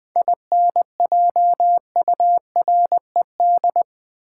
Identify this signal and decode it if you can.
Morse code, heard as a keyed tone, about 20 wpm, 700 Hz. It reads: INJURED
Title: 7 Letter Words - First 3 Letters Given - 20wpm